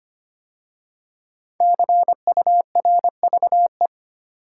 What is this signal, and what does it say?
Morse code, heard as a keyed tone, about 25 wpm, 700 Hz. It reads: CURVE
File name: 5 Letter Words - First 3 Letters Given - 25wpm